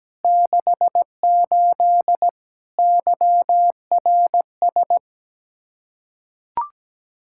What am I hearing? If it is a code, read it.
Morse code, heard as a keyed tone, about 17 wpm, 700 Hz. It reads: 68 YRS
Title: K1USN SST QSO Practice 17wpm